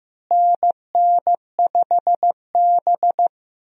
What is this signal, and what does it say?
Morse code, heard as a keyed tone, about 15 wpm, 700 Hz. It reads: NN5B
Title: USA Call Signs - 4 Characters - Repeated 2x - 15wpm